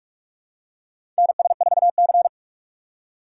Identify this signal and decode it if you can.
Morse code, heard as a keyed tone, about 43 wpm, 700 Hz. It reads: NS4C
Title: USA Call Signs - 4 Characters - 2x Character Spacing - 45wpm